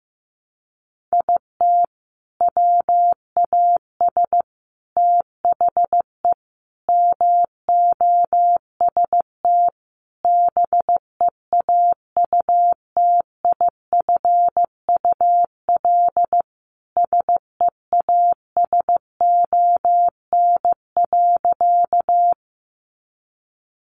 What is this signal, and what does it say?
Morse code, heard as a keyed tone, about 15 wpm, 700 Hz. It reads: IT WAS THE MOST BEAUTIFUL SEASON.
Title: Sentences from Top 1400 Words 15wpm